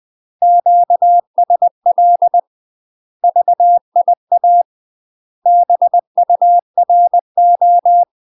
Morse code, heard as a keyed tone, about 20 wpm, 700 Hz. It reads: QSL VIA BURO
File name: UK MASTER QSOs TRAINING/20WPM